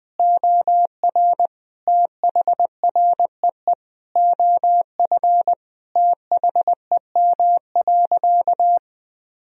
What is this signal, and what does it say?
Morse code, heard as a keyed tone, about 20 wpm, 700 Hz. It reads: OR THREE OF THEM.
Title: Sentences from Top 1000 Words - Review Part 1 - 20wpm